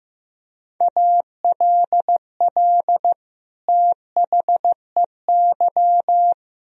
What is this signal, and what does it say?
Morse code, heard as a keyed tone, about 15 wpm, 700 Hz. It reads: ALL THEY
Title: Sentences from Top 1000 Words - Review Part 1 - 15wpm